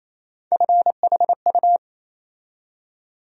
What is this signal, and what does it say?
Morse code, heard as a keyed tone, about 28 wpm, 700 Hz. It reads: FHU
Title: USA Call Sign Postfixes - 3 Letters 28wpm